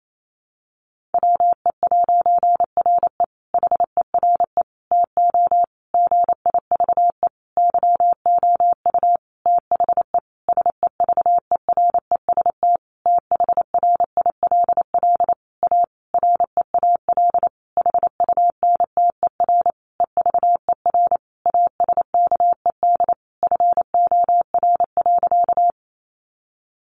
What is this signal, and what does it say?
Morse code, heard as a keyed tone, about 28 wpm, 700 Hz. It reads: WE'RE HERE TO GIVE YOU THE SEVEREST THRILL A REAL HUNTER EVER ASKED FOR.
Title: A Sound Of Thunder 28wpm